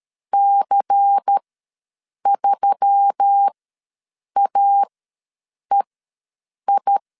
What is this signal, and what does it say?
Morse code, heard as a keyed tone, about 13 wpm, spaced as for 7 wpm, 800 Hz. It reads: C3AEI
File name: LEZIONI / 5 LBVC / 5 LBVC ws 12wpm 3159